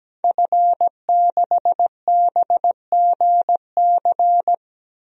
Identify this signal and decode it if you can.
Morse code, heard as a keyed tone, about 17 wpm, 700 Hz. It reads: F6BGC
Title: DXCC-100 Call Signs - 5 Characters - 17wpm